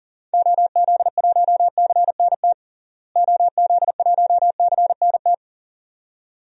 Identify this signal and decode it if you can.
Morse code, heard as a keyed tone, about 40 wpm, 700 Hz. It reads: OZ1CNT OZ1CNT
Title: DXCC-100 Call Signs - 6 Characters - Repeated 2x - 40wpm